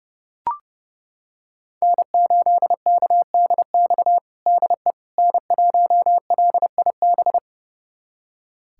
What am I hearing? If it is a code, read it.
Morse code, heard as a keyed tone, about 32 wpm, 700 Hz. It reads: N8KDX DE N1LIB
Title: USA Callsign DE Callsign - 5 Characters - 30wpm